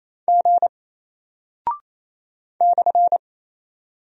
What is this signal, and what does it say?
Morse code, heard as a keyed tone, about 28 wpm, 700 Hz. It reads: G /
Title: Single Character (Letter:Number:Punctuation:Prosign) 28wpm